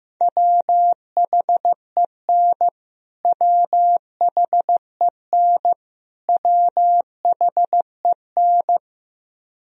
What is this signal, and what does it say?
Morse code, heard as a keyed tone, about 15 wpm, 700 Hz. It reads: WHEN WHEN WHEN
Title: Top 100 Words Repeated 15wpm